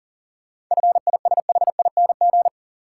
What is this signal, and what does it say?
Morse code, heard as a keyed tone, about 40 wpm, 700 Hz. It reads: FISHING
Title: Sentences from Top 1000 Words - Review Part 5 - New Word Drill - 40wpm